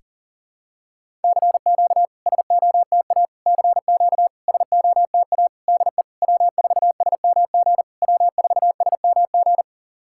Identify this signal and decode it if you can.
Morse code, heard as a keyed tone, about 38 wpm, 690 Hz. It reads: CQ SOTA CQ SOTA DE W4SMG W4SMG
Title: SOTA QSO Practice 40wpm